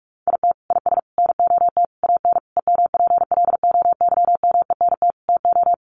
Morse code, heard as a keyed tone, about 44 wpm, 700 Hz. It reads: IT IS NOT AN EMPLOYMENT TO
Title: Sentences from Top 1500 Words 45wpm